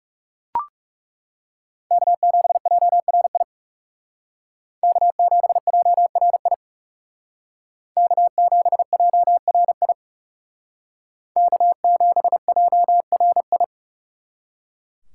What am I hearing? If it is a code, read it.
Morse code, heard as a keyed tone, about 38 wpm, 700 Hz. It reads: K7JRI K7JRI K7JRI K7JRI